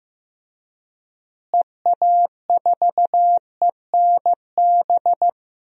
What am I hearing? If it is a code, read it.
Morse code, heard as a keyed tone, about 15 wpm, 700 Hz. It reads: EA4ENB
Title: DXCC-100 Call Signs - 6 Characters - Repeated 3x - 15wpm